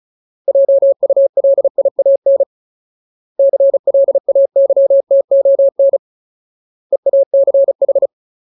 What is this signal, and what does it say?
Morse code, heard as a keyed tone, about 34 wpm, 550 Hz. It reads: JULIAN CLAYTON EACH
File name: Ultimate Word List / 35WPM Ultimate